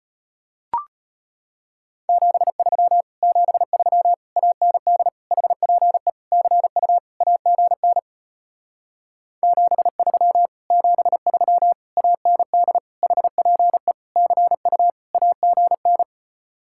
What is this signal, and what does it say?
Morse code, heard as a keyed tone, about 33 wpm, 700 Hz. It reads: 73 73 AND HPE CU AGN 73 73 AND HPE CU AGN